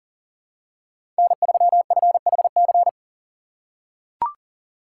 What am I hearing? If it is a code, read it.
Morse code, heard as a keyed tone, about 40 wpm, 700 Hz. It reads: N3FHC
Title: USA Call Signs - 5 Characters - 2x Character Spacing - 40wpm